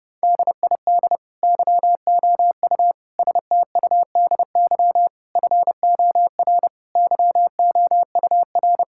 Morse code, heard as a keyed tone, about 30 wpm, 700 Hz. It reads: DID YOU STUDY FOR YOUR
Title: Sentences from Top 1000 Words - Review Part 6 - 30wpm